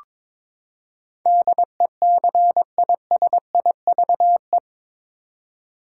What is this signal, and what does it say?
Morse code, heard as a keyed tone, about 22 wpm, 700 Hz. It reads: DECISIVE
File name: practice / 22wpm / Eight Letter Words - 22wpm